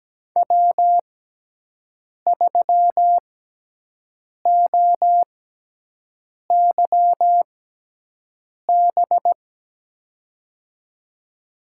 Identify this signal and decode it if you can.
Morse code, heard as a keyed tone, about 17 wpm, spaced as for 6 wpm, 700 Hz. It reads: W3OYB
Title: USA Call Signs - 5 Characters - 6x Character Spacing - 17wpm